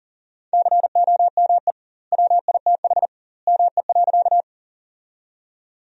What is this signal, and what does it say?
Morse code, heard as a keyed tone, about 40 wpm, 700 Hz. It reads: COME WITH ME.
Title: 2x Word-Spacing - Sentences from Top 200 Words - 40wpm